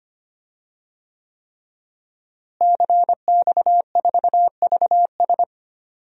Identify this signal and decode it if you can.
Morse code, heard as a keyed tone, about 25 wpm, 700 Hz. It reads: CX4VS
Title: DXCC-100 Call Signs - 5 Characters - Repeated 3x - 25wpm